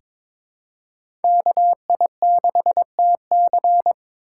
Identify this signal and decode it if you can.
Morse code, heard as a keyed tone, about 22 wpm, 700 Hz. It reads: KI6TC
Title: USA Call Signs - 2x Character Spacing - 22wpm